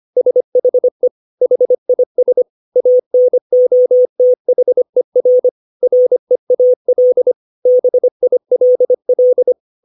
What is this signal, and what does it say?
Morse code, heard as a keyed tone, about 25 wpm, 500 Hz. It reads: SHE HIS ANOTHER REAL BILL
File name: 632 words / 632 Words 25WPM